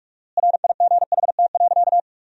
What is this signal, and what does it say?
Morse code, heard as a keyed tone, about 41 wpm, 700 Hz. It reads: RIGHT.
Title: 4x Word-Spacing - Sentences from Top 500 Words - 45wpm